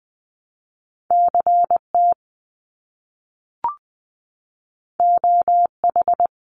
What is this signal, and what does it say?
Morse code, heard as a keyed tone, about 20 wpm, 700 Hz. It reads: CT OH
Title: US States Repeated 3x 20wpm